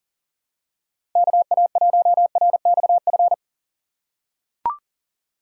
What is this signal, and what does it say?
Morse code, heard as a keyed tone, about 40 wpm, 700 Hz. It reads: KA1RXF
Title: USA Call Signs - 6 Characters - 6x Character Spacing - 40wpm